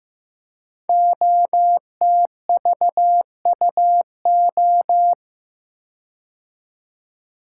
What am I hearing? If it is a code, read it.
Morse code, heard as a keyed tone, about 15 wpm, 700 Hz. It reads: OTVUO
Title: Code Groups - 5 Letter:Numbers - 15wpm